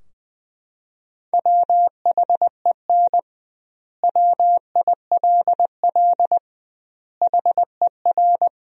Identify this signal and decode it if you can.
Morse code, heard as a keyed tone, about 21 wpm, 700 Hz. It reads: WHEN WILL HER
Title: 2x Word-Spacing - Sentences from Top 200 Words - 20wpm